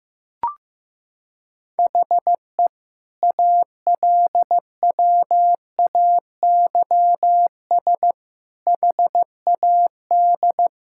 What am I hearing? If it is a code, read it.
Morse code, heard as a keyed tone, about 15 wpm, 700 Hz. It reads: HE ALWAYS HAD
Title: Sentences from Top 1600 Words 15wpm